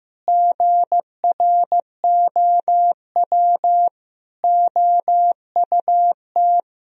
Morse code, heard as a keyed tone, about 15 wpm, 700 Hz. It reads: GROW OUT
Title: Sentences from Top 1000 Words - Review Part 1 - 15wpm